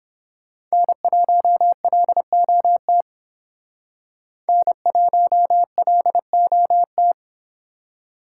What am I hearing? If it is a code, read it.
Morse code, heard as a keyed tone, about 30 wpm, 700 Hz. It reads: N1LOT N1LOT